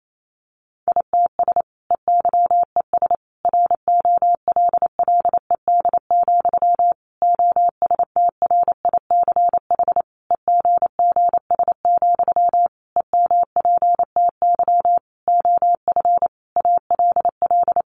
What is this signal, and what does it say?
Morse code, heard as a keyed tone, about 28 wpm, 700 Hz. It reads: ITS EYES ROLLED, OSTRICH EGGS, EMPTY OF ALL
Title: A Sound Of Thunder 28wpm